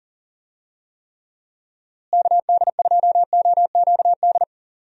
Morse code, heard as a keyed tone, about 40 wpm, 700 Hz. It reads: KD2OQD
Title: USA Call Signs - 6 Characters - Repeated 3x - 40wpm